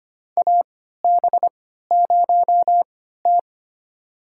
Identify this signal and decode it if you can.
Morse code, heard as a keyed tone, about 25 wpm, spaced as for 14 wpm, 700 Hz. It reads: AB0T
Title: USA Call Signs - 4 Characters - 3x Character Spacing - 25wpm